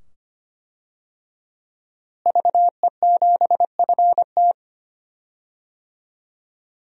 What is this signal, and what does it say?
Morse code, heard as a keyed tone, about 25 wpm, 700 Hz. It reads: VE7FT
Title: DXCC-100 Call Signs - 5 Characters - Repeated 2x - 25wpm